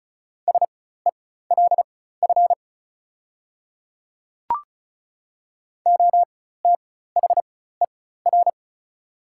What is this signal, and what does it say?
Morse code, heard as a keyed tone, about 34 wpm, spaced as for 16 wpm, 690 Hz. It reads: SELF OTHER
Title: 4x Character Spacing - Top 200 Words - 35wpm